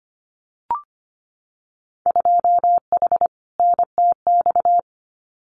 Morse code, heard as a keyed tone, about 25 wpm, 700 Hz. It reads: 2H NTX
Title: Winter Field Day Exchanges Encore 25wpm